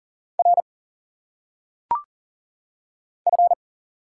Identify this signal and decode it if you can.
Morse code, heard as a keyed tone, about 40 wpm, 700 Hz. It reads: R F
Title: Single Letters - Phonetic Pronunciation 40wpm